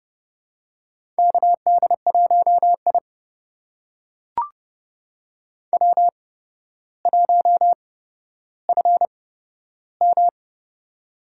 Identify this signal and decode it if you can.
Morse code, heard as a keyed tone, about 32 wpm, spaced as for 13 wpm, 700 Hz. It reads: KD1I W1FM
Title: USA Call Signs - 4 Characters - 8x Character Spacing - 30wpm